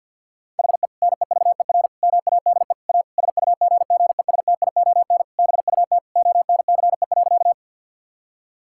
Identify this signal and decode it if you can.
Morse code, heard as a keyed tone, about 50 wpm, 700 Hz. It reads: HE NEVER MADE A SUGGESTION BUT ONCE.
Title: Sentences from Top 1600 Words 50wpm